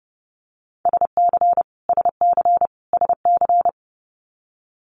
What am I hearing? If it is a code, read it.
Morse code, heard as a keyed tone, about 30 wpm, 700 Hz. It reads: SC SC SC
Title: US States Repeated 3x 30wpm